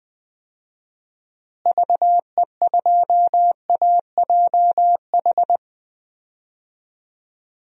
Morse code, heard as a keyed tone, about 20 wpm, 700 Hz. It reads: VE2AJH
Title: DXCC-100 Call Signs - 6 Characters - Repeated 2x - 20wpm